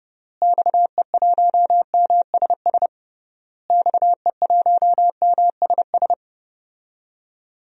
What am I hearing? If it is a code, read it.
Morse code, heard as a keyed tone, about 30 wpm, 700 Hz. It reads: XE1MSS XE1MSS
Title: DXCC-100 Call Signs - 6 Characters - Repeated 3x - 30wpm